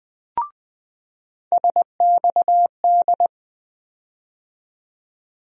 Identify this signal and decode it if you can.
Morse code, heard as a keyed tone, about 20 wpm, 700 Hz. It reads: SXD
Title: USA Call Sign Postfixes - 3 Letters 20wpm